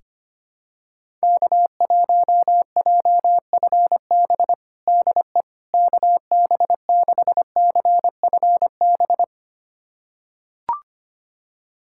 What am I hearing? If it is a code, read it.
Morse code, heard as a keyed tone, about 25 wpm, 700 Hz. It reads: K1JFB DE KB6CFB
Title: USA Callsign DE Callsign 25wpm